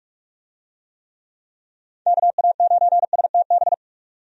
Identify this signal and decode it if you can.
Morse code, heard as a keyed tone, about 46 wpm, 700 Hz. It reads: KA9STB
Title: USA Call Signs - 6 Characters - 45wpm